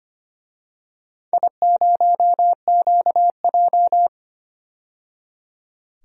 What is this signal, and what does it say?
Morse code, heard as a keyed tone, about 25 wpm, 700 Hz. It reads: I0QJ
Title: DXCC-100 Call Signs - 4 Characters - 25wpm